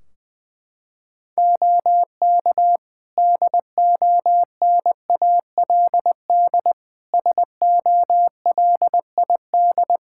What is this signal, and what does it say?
Morse code, heard as a keyed tone, about 20 wpm, 700 Hz. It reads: OK DONALD SOLID
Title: Practice QSO - Exchange 3 - More Info - 20wpm